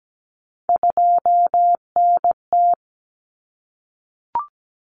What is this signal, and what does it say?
Morse code, heard as a keyed tone, about 17 wpm, 700 Hz. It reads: 2NT
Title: Common Cut Numbers - 3-Digit 17wpm